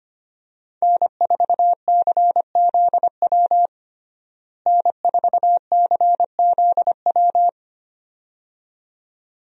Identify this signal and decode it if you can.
Morse code, heard as a keyed tone, about 25 wpm, 700 Hz. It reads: N4CZW N4CZW
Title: USA Call Signs - 5 Characters - Repeated 2x - 25wpm